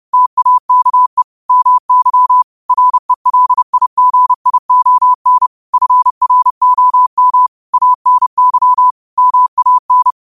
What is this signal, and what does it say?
Morse code, heard as a keyed tone, about 30 wpm, 1000 Hz. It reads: TAKE MY RELIGION FROM ANY MAN